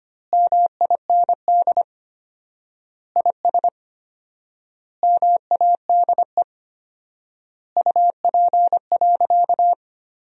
Morse code, heard as a keyed tone, about 25 wpm, 700 Hz. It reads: MIND IS MADE UP.
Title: 4x Word-Spacing - Sentences from Top 500 Words - 25wpm